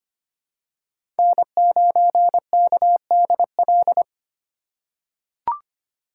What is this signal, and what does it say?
Morse code, heard as a keyed tone, about 25 wpm, 700 Hz. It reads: N9KDL
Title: USA Call Signs - 5 Characters - 4x Character Spacing - 25wpm